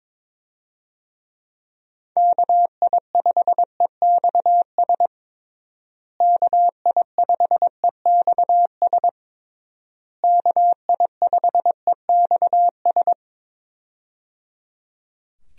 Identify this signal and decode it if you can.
Morse code, heard as a keyed tone, about 22 wpm, 700 Hz. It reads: KI5EXS KI5EXS KI5EXS
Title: USA Call Signs - 6 Characters - Repeated 3x - 22wpm